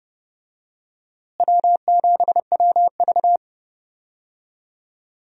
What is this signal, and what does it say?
Morse code, heard as a keyed tone, about 31 wpm, 700 Hz. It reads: W7WV
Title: USA Call Signs - 4 Characters - Repeated 3x - 30wpm